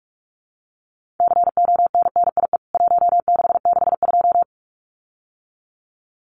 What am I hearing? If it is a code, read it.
Morse code, heard as a keyed tone, about 44 wpm, 700 Hz. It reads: CONNIE 1662
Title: CWT Exchange 45wpm